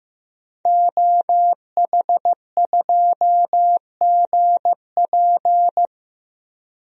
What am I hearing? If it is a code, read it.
Morse code, heard as a keyed tone, about 15 wpm, 700 Hz. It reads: OH2GP
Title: DXCC-100 Call Signs - 5 Characters - Repeated 2x - 15wpm